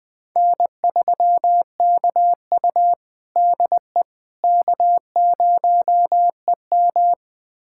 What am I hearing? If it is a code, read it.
Morse code, heard as a keyed tone, about 20 wpm, 700 Hz. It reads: N3KU DE K0EM
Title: USA Callsign DE Callsign - 4 Characters - 20wpm